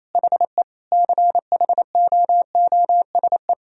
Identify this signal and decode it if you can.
Morse code, heard as a keyed tone, about 28 wpm, 700 Hz. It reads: HE CHOOSE
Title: Sentences from Top 1000 Words - Review Part 4 - 28wpm